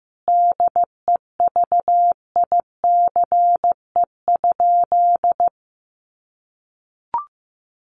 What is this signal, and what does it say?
Morse code, heard as a keyed tone, about 15 wpm, 700 Hz. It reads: DEVICE?
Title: Sentences from Top 1500 Words 15wpm